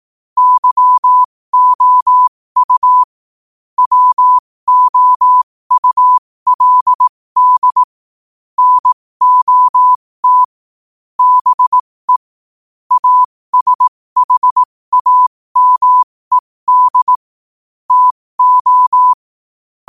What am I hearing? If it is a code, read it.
Morse code, heard as a keyed tone, about 18 wpm, spaced as for 15 wpm, 1000 Hz. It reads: YOU WOULD NOT BE ASHAMED TO
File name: qotdfiles / 15wpm